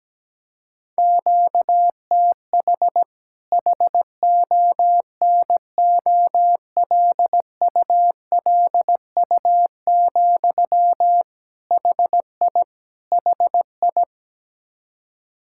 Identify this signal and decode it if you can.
Morse code, heard as a keyed tone, about 17 wpm, 700 Hz. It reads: QTH HONOLULU, HI HI
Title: Practice QSO - Exchange 1 - Beginning - 17wpm